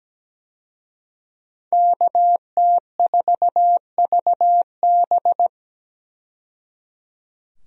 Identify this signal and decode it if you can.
Morse code, heard as a keyed tone, about 17 wpm, 700 Hz. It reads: KT4VB
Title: USA Call Signs - 5 Characters - 17wpm